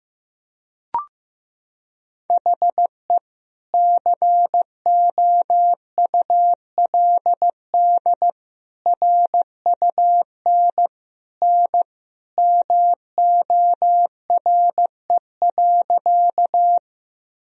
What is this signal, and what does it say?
Morse code, heard as a keyed tone, about 15 wpm, 700 Hz. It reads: HE COULD RUN N MORE.
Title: Sentences from Top 300 Words and Abbreviations 15wpm